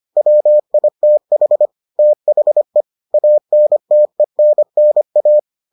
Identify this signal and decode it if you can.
Morse code, heard as a keyed tone, about 25 wpm, 600 Hz. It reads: WITH THE ANTENNA